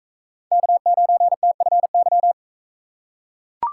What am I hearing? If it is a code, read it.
Morse code, heard as a keyed tone, about 40 wpm, 700 Hz. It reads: K9TFY